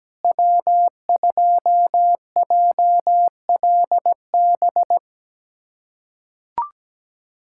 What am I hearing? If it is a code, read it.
Morse code, heard as a keyed tone, about 17 wpm, 700 Hz. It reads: W2JLB
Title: USA Callsign DE Callsign - 5 Characters - 17wpm